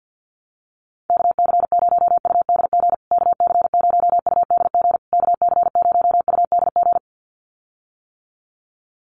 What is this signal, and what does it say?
Morse code, heard as a keyed tone, about 48 wpm, 700 Hz. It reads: KC0UDG KC0UDG KC0UDG
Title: USA Call Signs Repeated 3x 50wpm